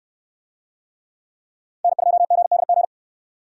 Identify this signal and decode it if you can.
Morse code, heard as a keyed tone, about 63 wpm, 700 Hz. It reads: N2GDG